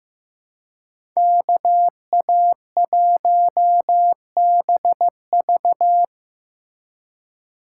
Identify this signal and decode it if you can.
Morse code, heard as a keyed tone, about 16 wpm, 700 Hz. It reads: KA1BV
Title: USA Call Signs - 5 Characters - 3x Character Spacing - 15wpm